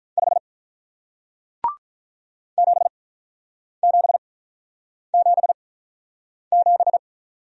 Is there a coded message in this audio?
Yes: Morse code from 0.2 to 7.0 s, heard as a keyed tone, about 48 wpm, spaced as for 9 wpm, 700 Hz. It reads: 5 7777